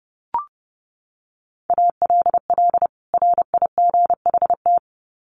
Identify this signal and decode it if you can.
Morse code, heard as a keyed tone, about 30 wpm, 700 Hz. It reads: ALL RIGHT
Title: Sets of 2 Words 30wpm